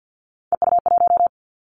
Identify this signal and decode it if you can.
Morse code, heard as a keyed tone, about 50 wpm, 700 Hz. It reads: EU1
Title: Callsign Prefixes for Top 100 DXCC - 50wpm